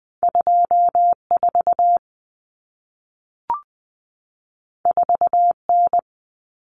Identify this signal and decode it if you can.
Morse code, heard as a keyed tone, about 20 wpm, 700 Hz. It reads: 24 4N
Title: Common Cut Numbers - 2-Digit 20wpm